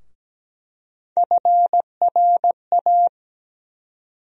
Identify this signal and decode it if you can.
Morse code, heard as a keyed tone, about 17 wpm, 700 Hz. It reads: FRA
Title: USA Call Sign Postfixes - 3 Letters 17wpm